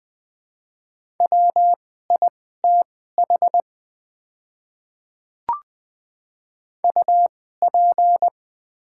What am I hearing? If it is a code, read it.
Morse code, heard as a keyed tone, about 20 wpm, spaced as for 14 wpm, 690 Hz. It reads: WITH UP
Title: 2x Character Spacing - Top 100 Words - 20wpm